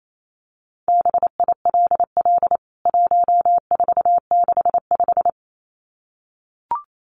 Morse code, heard as a keyed tone, about 28 wpm, 700 Hz. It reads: BILL 1465
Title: CWT Exchange 28wpm